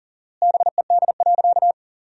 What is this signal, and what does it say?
Morse code, heard as a keyed tone, about 40 wpm, 700 Hz. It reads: BED.
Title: 4x Word-Spacing - Sentences from Top 500 Words - 40wpm